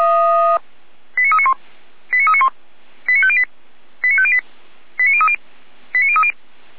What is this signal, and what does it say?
Five-tone selective call (ZVEI), heard as quick runs of five tones, tones of about 70 ms. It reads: 89391 89391 89508 89508 89030 89030
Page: SDR/ZVEI